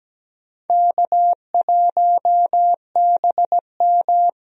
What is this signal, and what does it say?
Morse code, heard as a keyed tone, about 17 wpm, 700 Hz. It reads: K1BM
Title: USA Call Signs - 4 Characters - 17wpm